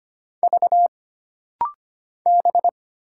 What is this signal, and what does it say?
Morse code, heard as a keyed tone, about 25 wpm, 700 Hz. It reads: V B
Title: Single Letters - Rapid-Fire 25wpm